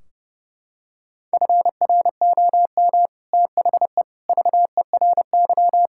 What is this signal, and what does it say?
Morse code, heard as a keyed tone, about 30 wpm, 700 Hz. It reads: FROM THE VERY
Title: Sentences from Top 1600 Words 30wpm